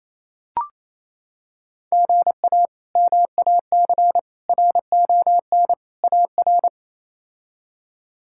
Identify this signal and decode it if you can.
Morse code, heard as a keyed tone, about 28 wpm, 700 Hz. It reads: GA MAC RON AR
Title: K1USN SST QSO Practice - 28wpm